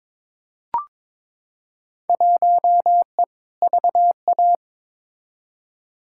Morse code, heard as a keyed tone, about 22 wpm, 700 Hz. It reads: 1E VA
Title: Field Day Exchanges - 22wpm